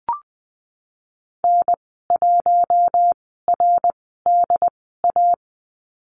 Morse code, heard as a keyed tone, about 20 wpm, spaced as for 15 wpm, 700 Hz. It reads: N1RDA
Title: 069 Lesson 12g - Letter d - Call Signs 20wpm